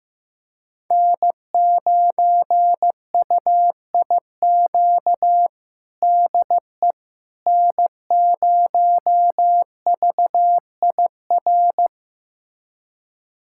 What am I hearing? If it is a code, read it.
Morse code, heard as a keyed tone, about 15 wpm, 700 Hz. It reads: N9UIQ DE N0VIR
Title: USA Callsign DE Callsign - 5 Characters - 15wpm